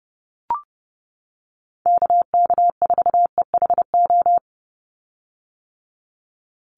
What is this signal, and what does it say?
Morse code, heard as a keyed tone, about 30 wpm, 700 Hz. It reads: KK4EHO
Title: USA Call Signs 30wpm